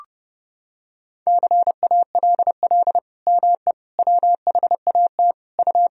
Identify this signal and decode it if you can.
Morse code, heard as a keyed tone, about 32 wpm, 700 Hz. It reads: CALL ME WHAT U
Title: Sentences from Top 200 Words and Abbreviations 30wpm